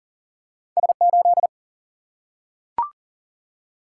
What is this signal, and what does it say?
Morse code, heard as a keyed tone, about 38 wpm, 700 Hz. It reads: S8
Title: Single-Letter Single-Number 40wpm